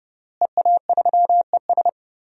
Morse code, heard as a keyed tone, about 32 wpm, 700 Hz. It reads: EA3ES
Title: DXCC-100 Call Signs - 5 Characters - Repeated 2x - 30wpm